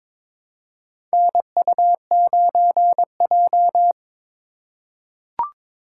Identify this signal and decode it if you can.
Morse code, heard as a keyed tone, about 22 wpm, 700 Hz. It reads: NU9J
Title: USA Call Signs - 4 Characters - 4x Character Spacing - 22wpm